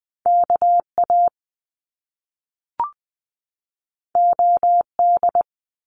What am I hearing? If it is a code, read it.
Morse code, heard as a keyed tone, about 20 wpm, 700 Hz. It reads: KA OD
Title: Top Two Letter Combinations in Words 20wpm